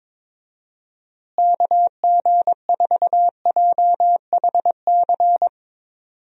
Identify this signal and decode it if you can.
Morse code, heard as a keyed tone, about 21 wpm, 700 Hz. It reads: KG4JHC
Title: USA Call Signs - 6 Characters - Repeated 2x - 22wpm